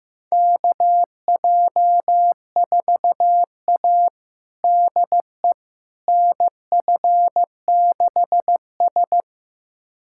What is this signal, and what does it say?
Morse code, heard as a keyed tone, about 15 wpm, 700 Hz. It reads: KJ4A DE NF6S
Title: USA Callsign DE Callsign - 4 Characters - 15wpm